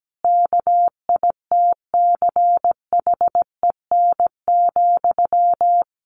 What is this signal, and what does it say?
Morse code, heard as a keyed tone, about 17 wpm, 700 Hz. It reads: KITCHEN,
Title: Sentences from Top 1400 Words 17wpm